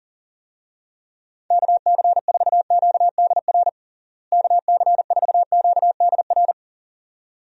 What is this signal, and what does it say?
Morse code, heard as a keyed tone, about 37 wpm, 700 Hz. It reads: KC4QDR KC4QDR
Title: USA Call Signs - 6 Characters - Repeated 2x - 40wpm